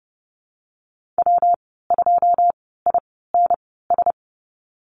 Morse code, heard as a keyed tone, about 30 wpm, spaced as for 17 wpm, 700 Hz. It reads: W2INS
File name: Lesson 19g - Letter w - Call Signs 30wpm